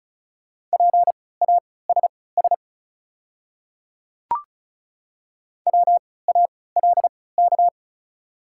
Morse code, heard as a keyed tone, about 34 wpm, spaced as for 20 wpm, 690 Hz. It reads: PASS WALK
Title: 3x Character Spacing - Top 500 Words - 35wpm